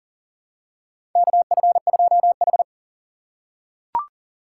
Morse code, heard as a keyed tone, about 40 wpm, 700 Hz. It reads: KF2H
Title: USA Call Signs - 4 Characters - 4x Character Spacing - 40wpm